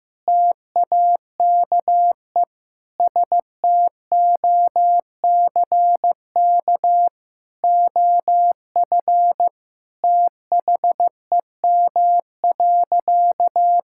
Morse code, heard as a keyed tone, about 15 wpm, 700 Hz. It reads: TAKE STOCK OF THEM.
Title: Sentences from Top 1000 Words - Review Part 5 - 15wpm